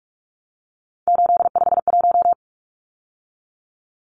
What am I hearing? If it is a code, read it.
Morse code, heard as a keyed tone, about 44 wpm, 700 Hz. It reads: 851
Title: Numbers - 3-Digit 45wpm